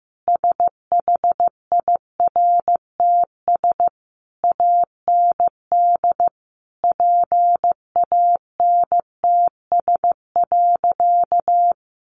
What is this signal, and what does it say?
Morse code, heard as a keyed tone, about 15 wpm, 700 Hz. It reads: SHIRTS AND PANTS.
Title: A Sound Of Thunder 15wpm